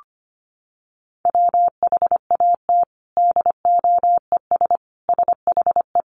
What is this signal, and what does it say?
Morse code, heard as a keyed tone, about 25 wpm, 700 Hz. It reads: WHAT DOES SHE
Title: Sentences from Top 1500 Words 25wpm